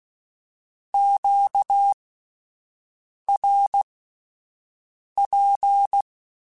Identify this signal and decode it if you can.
Morse code, heard as a keyed tone, about 16 wpm, spaced as for 5 wpm, 800 Hz. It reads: QRP
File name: (spaced) / Q-Calls Spaced-mp3s / Q Calls sp Once @10